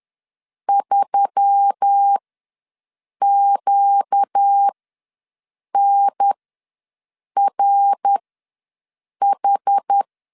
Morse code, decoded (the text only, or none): 3QNRH